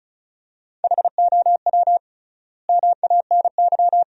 Morse code, heard as a keyed tone, about 34 wpm, 700 Hz. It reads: HOW MANY
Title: 3x Word-Spacing - Sentences from Top 400 Words - 35wpm